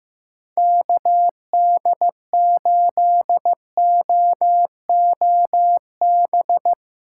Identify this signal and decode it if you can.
Morse code, heard as a keyed tone, about 18 wpm, 700 Hz. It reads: KD8OOB
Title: Speed Racing - USA 2x3 Call Signs 15wpm